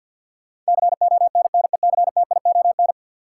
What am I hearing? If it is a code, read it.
Morse code, heard as a keyed tone, about 51 wpm, 700 Hz. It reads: CONNECTION
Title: Sentences from Top 1300 Words - New Word Drill - 50wpm